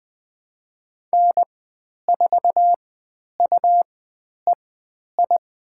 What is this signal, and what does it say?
Morse code, heard as a keyed tone, about 20 wpm, spaced as for 10 wpm, 700 Hz. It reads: N4UEI
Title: 109 Lesson 18g - Letter m - Call Signs 20wpm